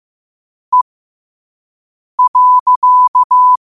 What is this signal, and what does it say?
Morse code, heard as a keyed tone, about 15 wpm, spaced as for 5 wpm, 1000 Hz. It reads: E.
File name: qotdfiles / 5wpm